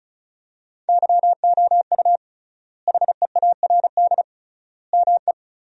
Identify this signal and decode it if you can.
Morse code, heard as a keyed tone, about 34 wpm, 700 Hz. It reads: YOU HEARD ME
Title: 3x Word-Spacing - Sentences from Top 500 Words - 35wpm